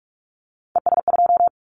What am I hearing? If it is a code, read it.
Morse code, heard as a keyed tone, about 46 wpm, 700 Hz. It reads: ES2
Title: Callsign Prefixes for Top 50 DXCC - 45wpm